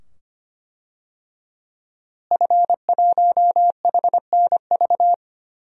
Morse code, heard as a keyed tone, about 25 wpm, 700 Hz. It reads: F1HNV
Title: DXCC-100 Call Signs - 5 Characters - Repeated 2x - 25wpm